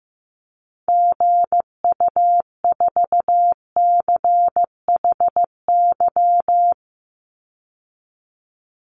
Morse code, heard as a keyed tone, about 15 wpm, 700 Hz. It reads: GU4CHY
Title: Call Signs Repeated 3x - 15wpm